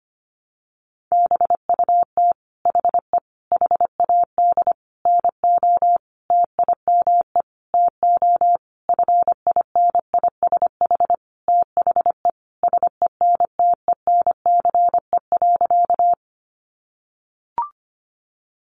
Morse code, heard as a keyed tone, about 25 wpm, 700 Hz. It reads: BUT HE HAD NO TIME TO FINISH THE SENTENCE.
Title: Sentences from Top 1200 Words 25wpm